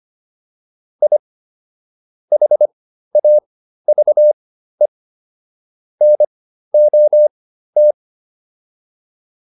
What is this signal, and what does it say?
Morse code, heard as a keyed tone, about 25 wpm, spaced as for 13 wpm, 600 Hz. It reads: I HAVE NOT